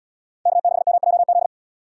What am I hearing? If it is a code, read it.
Morse code, heard as a keyed tone, about 72 wpm, 700 Hz. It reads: K6MJZ